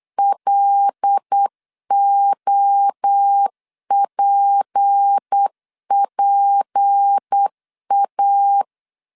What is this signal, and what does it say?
Morse code, heard as a keyed tone, about 8 wpm, 800 Hz. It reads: LOPPA
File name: LEZIONI / 6 PKX0 / 6 P 8wpm 3025